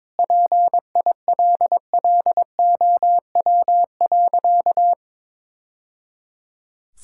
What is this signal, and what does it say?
Morse code, heard as a keyed tone, about 21 wpm, 690 Hz. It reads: PILLOW.
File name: practice / 22wpm / Easy Sentences with Hints - 22wpm